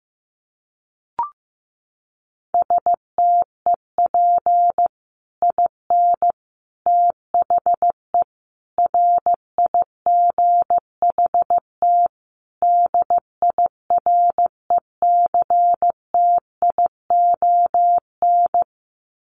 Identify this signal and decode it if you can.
Morse code, heard as a keyed tone, about 15 wpm, 700 Hz. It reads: STEP IN THE RIGHT DIRECTION
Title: Sets of 5 Words Encore 15wpm